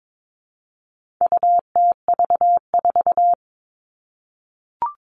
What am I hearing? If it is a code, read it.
Morse code, heard as a keyed tone, about 22 wpm, 700 Hz. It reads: UTV4
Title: All Cut Numbers - 4-Digit 22wpm